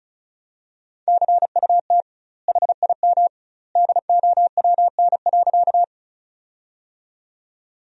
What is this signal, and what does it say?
Morse code, heard as a keyed tone, about 34 wpm, 700 Hz. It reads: CUT HIM DOWN.
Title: 2x Word-Spacing - Sentences from Top 300 Words - 35wpm